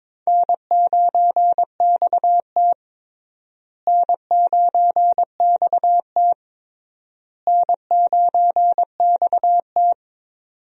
Morse code, heard as a keyed tone, about 22 wpm, 700 Hz. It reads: N9XT N9XT N9XT
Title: USA Call Signs - 4 Characters - Repeated 3x - 22wpm